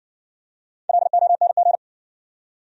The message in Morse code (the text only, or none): LONG